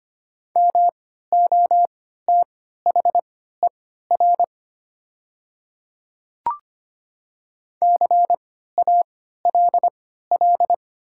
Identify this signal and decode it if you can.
Morse code, heard as a keyed tone, about 25 wpm, spaced as for 14 wpm, 700 Hz. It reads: MOTHER CALL
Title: 3x Character Spacing - Top 200 Words - 25wpm